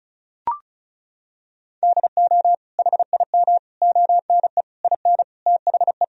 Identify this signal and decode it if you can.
Morse code, heard as a keyed tone, about 36 wpm, 700 Hz. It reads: DO HIM ONE IN THE
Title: Sentences from Top 1000 Words - Review Part 1 - 35wpm